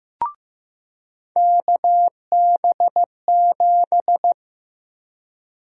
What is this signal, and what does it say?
Morse code, heard as a keyed tone, about 15 wpm, 700 Hz. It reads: KB7
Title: USA Call Sign Prefixes - Repeated 2x - 15wpm